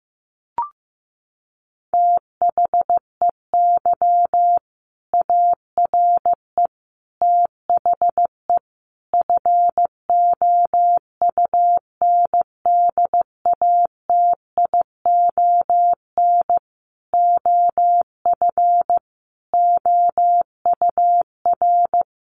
Morse code, heard as a keyed tone, about 15 wpm, 700 Hz. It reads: THEY ARE THE FOUNDATION OF OUR
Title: Sentences from Top 1200 Words 15wpm